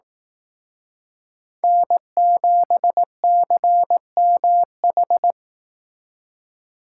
Morse code, heard as a keyed tone, about 18 wpm, 700 Hz. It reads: N7CMH
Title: Speed Racing - USA 1x3 Call Signs 15wpm